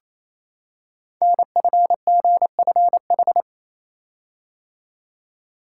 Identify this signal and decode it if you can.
Morse code, heard as a keyed tone, about 27 wpm, 700 Hz. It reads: NFGFH
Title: Code Groups - 5 Letters - 28wpm